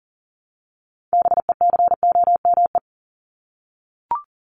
Morse code, heard as a keyed tone, about 40 wpm, 700 Hz. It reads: BECOME
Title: Top 501-750 Words 40wpm